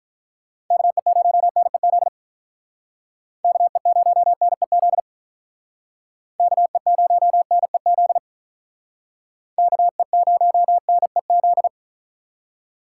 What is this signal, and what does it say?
Morse code, heard as a keyed tone, about 45 wpm, 700 Hz. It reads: KE0NEZ KE0NEZ KE0NEZ KE0NEZ